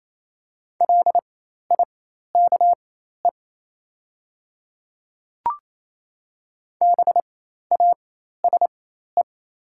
Morse code, heard as a keyed tone, about 27 wpm, spaced as for 13 wpm, 700 Hz. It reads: LIKE BASE
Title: 4x Character Spacing - Top 300 Words - 28wpm